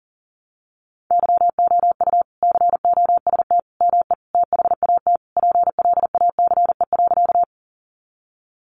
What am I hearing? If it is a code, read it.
Morse code, heard as a keyed tone, about 40 wpm, 700 Hz. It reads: YOU COST ME THAT PLACE.
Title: Sentences from Top 800 Words 40wpm